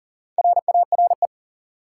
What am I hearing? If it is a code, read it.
Morse code, heard as a keyed tone, about 37 wpm, 700 Hz. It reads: RARE
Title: Sentences from Top 1100 Words - New Word Drill - 40wpm